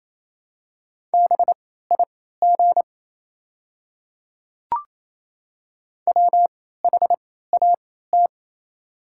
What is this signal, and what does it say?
Morse code, heard as a keyed tone, about 27 wpm, spaced as for 16 wpm, 690 Hz. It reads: BIG WHAT
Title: 3x Character Spacing - Top 300 Words - 28wpm